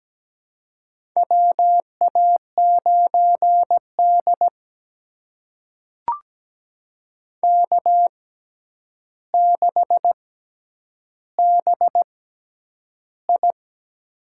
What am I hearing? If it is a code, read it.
Morse code, heard as a keyed tone, about 17 wpm, spaced as for 9 wpm, 700 Hz. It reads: WA9D K6BI
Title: USA Call Signs - 4 Characters - 6x Character Spacing - 17wpm